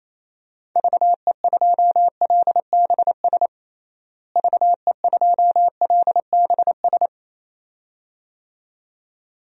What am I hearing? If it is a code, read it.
Morse code, heard as a keyed tone, about 28 wpm, 700 Hz. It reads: VE2LBS VE2LBS
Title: DXCC-100 Call Signs - 6 Characters - Repeated 2x - 28wpm